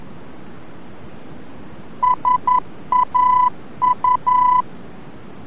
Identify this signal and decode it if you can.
Morse code, heard as a keyed tone, about 11 wpm, 1000 Hz. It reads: SAU
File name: france / mx / SAU-VOR-F